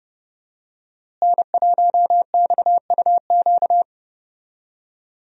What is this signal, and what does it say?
Morse code, heard as a keyed tone, about 28 wpm, 700 Hz. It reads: N1XUQ
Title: USA Call Signs - 5 Characters - Repeated 3x - 30wpm